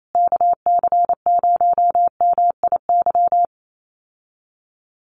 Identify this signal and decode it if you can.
Morse code, heard as a keyed tone, about 28 wpm, 700 Hz. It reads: KC0MIY
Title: USA Call Signs Repeated 3x 28wpm